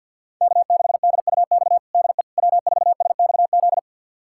Morse code, heard as a keyed tone, about 52 wpm, 700 Hz. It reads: K6DUX DE W4SXZ
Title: USA Callsign DE Callsign - 5 Characters - 50wpm